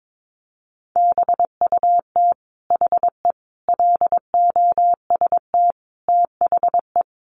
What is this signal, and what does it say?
Morse code, heard as a keyed tone, about 22 wpm, 700 Hz. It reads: BUT HE LOST THE
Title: Sentences from Top 1500 Words 22wpm